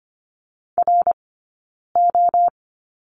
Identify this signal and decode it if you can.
Morse code, heard as a keyed tone, about 25 wpm, 700 Hz. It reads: R O
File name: Lesson 17d - Letter u - Two Characters 25wpm